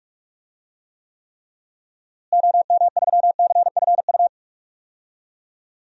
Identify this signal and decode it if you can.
Morse code, heard as a keyed tone, about 43 wpm, 700 Hz. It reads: OM3CFU
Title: DXCC-100 Call Signs - 6 Characters - Repeated 3x - 45wpm